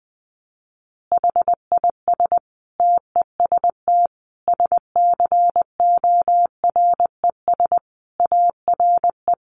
Morse code, heard as a keyed tone, about 20 wpm, 700 Hz. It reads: HIS TEST SCORES ARE
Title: Additional Easy Phrases 20wpm